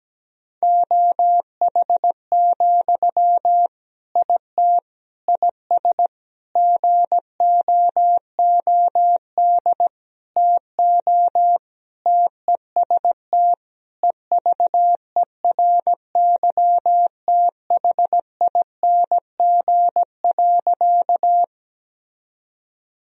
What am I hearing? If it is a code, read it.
Morse code, heard as a keyed tone, about 17 wpm, 700 Hz. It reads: OH, IT IS GOOD TO TEST EVERYTHING.
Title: Sentences from Top 1000 Words - Review Part 3 - 17wpm